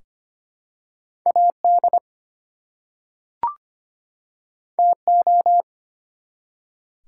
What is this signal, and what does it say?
Morse code, heard as a keyed tone, about 25 wpm, 700 Hz. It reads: AD TO
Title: Two Letter Words - 25wpm